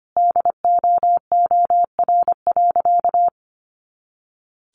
Morse code, heard as a keyed tone, about 25 wpm, 700 Hz. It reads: DOOR.
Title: Sentences from Top 300 Words 25wpm